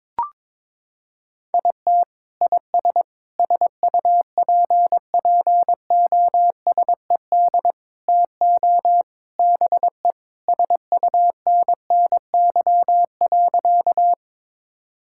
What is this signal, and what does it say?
Morse code, heard as a keyed tone, about 22 wpm, 700 Hz. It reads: IT IS SUPPOSED TO BE SUNNY.
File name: practice / 22wpm / Sentences with 3-Word Phrases - 22wpm